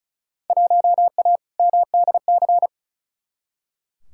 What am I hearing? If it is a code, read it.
Morse code, heard as a keyed tone, about 36 wpm, 700 Hz. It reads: 1A MDC
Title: Field Day Exchanges - 35wpm